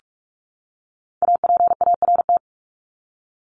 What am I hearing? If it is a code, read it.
Morse code, heard as a keyed tone, about 44 wpm, 700 Hz. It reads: APART
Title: Top 1501-1750 Words 45wpm